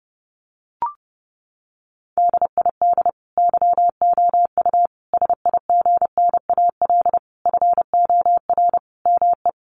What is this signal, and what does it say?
Morse code, heard as a keyed tone, about 30 wpm, 700 Hz. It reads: DID YOU SIGNAL FOR ME
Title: Sentences from Top 1400 Words 30wpm